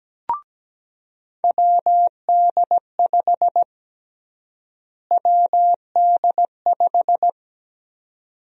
Morse code, heard as a keyed tone, about 17 wpm, 700 Hz. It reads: WD5 WD5
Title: USA Call Sign Prefixes - Repeated 2x - 17wpm